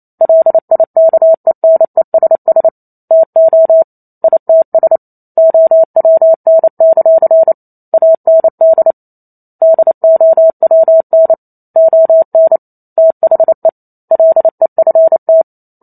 Morse code, heard as a keyed tone, about 28 wpm, 650 Hz. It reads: LIKENESS TO ITS OWN; AND DOWN ON THE LEFT